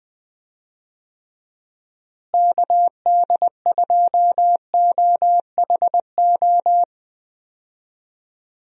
Morse code, heard as a keyed tone, about 20 wpm, 700 Hz. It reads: KD2OHO
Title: USA Call Signs - 6 Characters - Repeated 2x - 20wpm